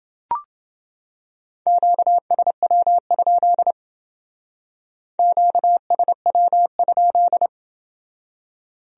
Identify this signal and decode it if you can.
Morse code, heard as a keyed tone, about 29 wpm, 700 Hz. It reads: QSW? QSW?